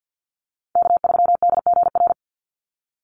Morse code, heard as a keyed tone, about 50 wpm, 700 Hz. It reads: K3DGR
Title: Call Signs - 50wpm